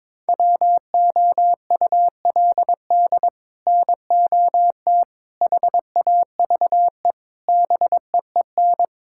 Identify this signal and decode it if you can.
Morse code, heard as a keyed tone, about 22 wpm, 700 Hz. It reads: WOULD NOT HAVE BEEN
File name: practice / 22wpm / Sets of 6 Words - 22wpm